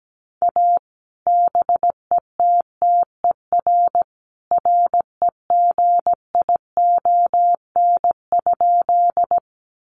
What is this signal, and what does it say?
Morse code, heard as a keyed tone, about 17 wpm, 700 Hz. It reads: A BETTER REGION?
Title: Sentences from Top 1500 Words 17wpm